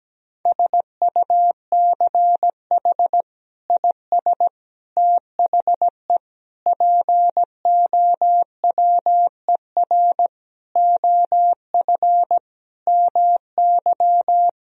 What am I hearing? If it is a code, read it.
Morse code, heard as a keyed tone, about 17 wpm, 700 Hz. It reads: SUCH IS THE POWER OF MY
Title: Sentences from Top 1000 Words - Review Part 2 - 17wpm